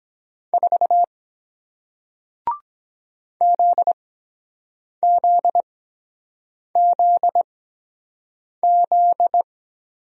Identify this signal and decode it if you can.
Morse code, heard as a keyed tone, about 23 wpm, spaced as for 7 wpm, 700 Hz. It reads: 4 ZZZZ